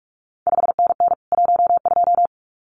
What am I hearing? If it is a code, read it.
Morse code, heard as a keyed tone, about 46 wpm, 700 Hz. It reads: 5NN 12
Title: CQ WW DX Contest Exchange - 45wpm